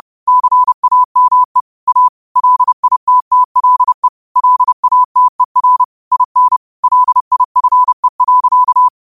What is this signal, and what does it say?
Morse code, heard as a keyed tone, about 30 wpm, 1000 Hz. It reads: CAME A LITTLE LATER IN LIFE.
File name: qotdfiles / 30wpm